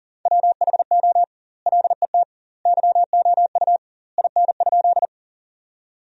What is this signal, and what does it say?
Morse code, heard as a keyed tone, about 40 wpm, 700 Hz. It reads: WHO LET YOU IN?
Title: 2x Word-Spacing - Sentences from Top 300 Words - 40wpm